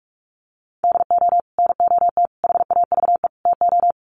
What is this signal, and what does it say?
Morse code, heard as a keyed tone, about 44 wpm, 700 Hz. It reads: DO NOT HAVE TO
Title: Sets of 4 Words 45wpm